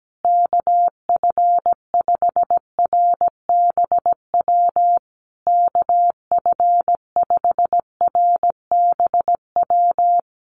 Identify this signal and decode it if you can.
Morse code, heard as a keyed tone, about 17 wpm, 700 Hz. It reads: KF5RBW KF5RBW
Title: USA Call Signs Repeated 3x 17wpm